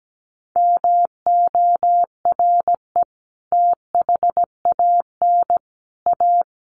Morse code, heard as a keyed tone, about 17 wpm, 700 Hz. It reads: MORE THAN A
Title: Sets of 5 Words Encore2 17wpm